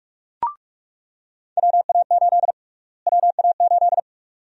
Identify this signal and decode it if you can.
Morse code, heard as a keyed tone, about 42 wpm, 700 Hz. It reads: WA8 WA8
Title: USA Call Sign Prefixes - Repeated 2x - 45wpm